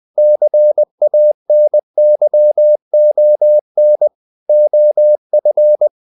Morse code, heard as a keyed tone, about 20 wpm, 600 Hz. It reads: CANYON OF
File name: the old man and the sea part 2 20wpm